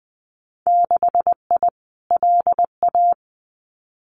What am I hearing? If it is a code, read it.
Morse code, heard as a keyed tone, about 20 wpm, 700 Hz. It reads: 6I LA
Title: Winter Field Day Exchanges Encore 20wpm